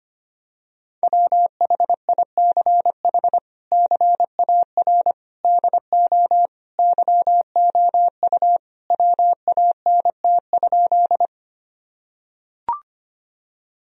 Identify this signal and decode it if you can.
Morse code, heard as a keyed tone, about 25 wpm, 700 Hz. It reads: WHICH CAR DO YOU WANT?
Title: Sentences from Top 1000 Words - Review Part 1 - 25wpm